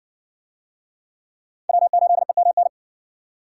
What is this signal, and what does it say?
Morse code, heard as a keyed tone, about 56 wpm, 700 Hz. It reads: W8EMN